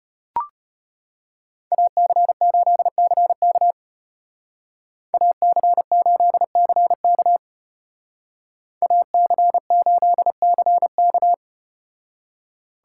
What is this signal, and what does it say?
Morse code, heard as a keyed tone, about 34 wpm, 700 Hz. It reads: AC8CK AC8CK AC8CK